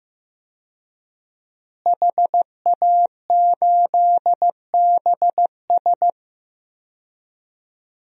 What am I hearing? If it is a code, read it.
Morse code, heard as a keyed tone, about 16 wpm, 700 Hz. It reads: HA8BS
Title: DXCC-100 Call Signs - 5 Characters - Repeated 2x - 15wpm